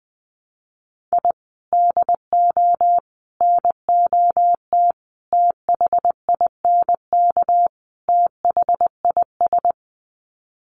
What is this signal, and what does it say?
Morse code, heard as a keyed tone, about 20 wpm, 700 Hz. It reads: I DO NOT THINK THIS
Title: Sets of 5 Words Encore 20wpm